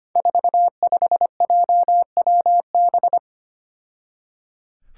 Morse code, heard as a keyed tone, about 25 wpm, 700 Hz. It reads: 45JWB
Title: Five Challenging Letter-Number Combinations 25wpm